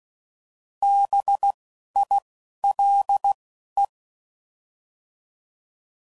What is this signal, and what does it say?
Morse code, heard as a keyed tone, about 16 wpm, 800 Hz. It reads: B I L E